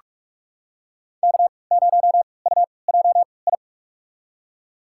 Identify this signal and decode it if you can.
Morse code, heard as a keyed tone, about 46 wpm, spaced as for 25 wpm, 700 Hz. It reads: K0UJI
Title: USA Call Signs - 5 Characters - 3x Character Spacing - 45wpm